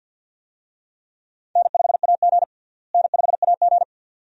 Morse code, heard as a keyed tone, about 45 wpm, 700 Hz. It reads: N5AG N5AG
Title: USA Call Signs - 4 Characters - Repeated 2x - 50wpm